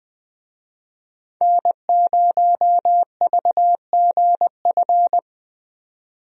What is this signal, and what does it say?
Morse code, heard as a keyed tone, about 20 wpm, 700 Hz. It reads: N0VGF
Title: USA Call Signs - 5 Characters - 20wpm